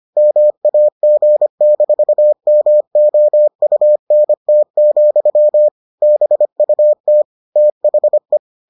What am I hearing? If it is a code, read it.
Morse code, heard as a keyed tone, about 25 wpm, 600 Hz. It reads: MAG-MOUNT, BUT THE